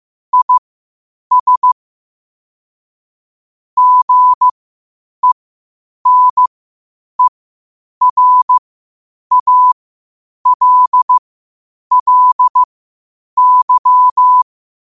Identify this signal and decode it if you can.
Morse code, heard as a keyed tone, about 15 wpm, spaced as for 8 wpm, 1000 Hz. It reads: IS GENERALLY
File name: qotdfiles / 8wpm